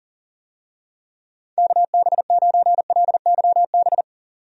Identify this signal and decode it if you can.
Morse code, heard as a keyed tone, about 40 wpm, 700 Hz. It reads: KB9LYB
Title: USA Call Signs - 6 Characters - 40wpm